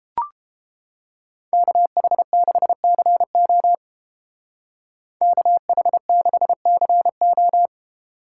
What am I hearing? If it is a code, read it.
Morse code, heard as a keyed tone, about 29 wpm, 700 Hz. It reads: KH6CO KH6CO